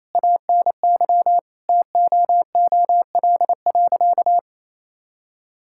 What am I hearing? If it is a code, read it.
Morse code, heard as a keyed tone, about 28 wpm, 700 Hz. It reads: ANY TOOL.
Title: Sentences from Top 1000 Words - Review Part 6 - 28wpm